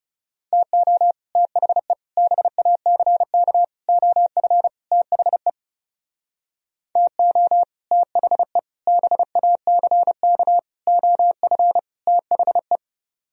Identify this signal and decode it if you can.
Morse code, heard as a keyed tone, about 39 wpm, 700 Hz. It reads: TO THE BACK OF THE TO THE BACK OF THE